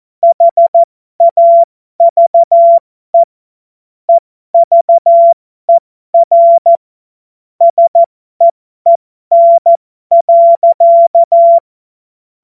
Morse code, heard as a keyed tone, about 14 wpm, spaced as for 12 wpm, 680 Hz. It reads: HAVE EVER SEEN.
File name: Morsekurs / Witze